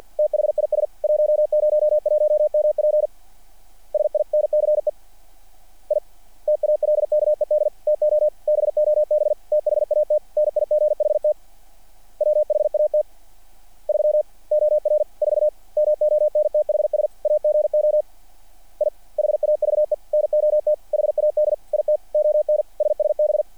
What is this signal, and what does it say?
Morse code, heard as a keyed tone, about 48 wpm, 600 Hz. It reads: THIS 101MP SINCE I TALKED TO BOB THAT NIGHT WHAT 3 OR 4 MONTHS AGO I HAVE NOT HAD IT ON SSB
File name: archived / CRSnet